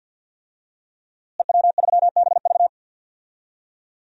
Morse code, heard as a keyed tone, about 47 wpm, 700 Hz. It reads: EW3BV
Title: DXCC-100 Call Signs - 5 Characters - Repeated 2x - 50wpm